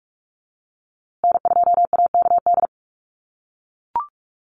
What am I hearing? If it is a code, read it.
Morse code, heard as a keyed tone, about 44 wpm, 700 Hz. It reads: N2AKD
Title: USA Call Signs Repeated 3x 45wpm